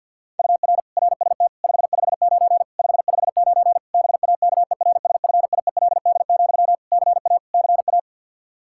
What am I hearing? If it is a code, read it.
Morse code, heard as a keyed tone, about 51 wpm, 700 Hz. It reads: UR RST 559 559 BAKERSFIELD, CA CA
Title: K1USN SST QSO Practice 50wpm